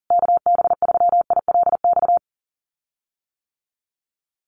Morse code, heard as a keyed tone, about 40 wpm, 700 Hz. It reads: KB3ILX
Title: USA Call Signs Repeated 3x 40wpm